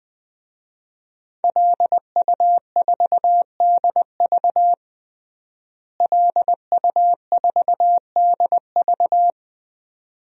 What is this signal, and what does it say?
Morse code, heard as a keyed tone, about 20 wpm, 700 Hz. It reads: LU4DV LU4DV
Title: DXCC-100 Call Signs - 5 Characters - Repeated 2x - 20wpm